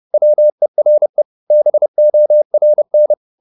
Morse code, heard as a keyed tone, about 30 wpm, 600 Hz. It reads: WERE BORN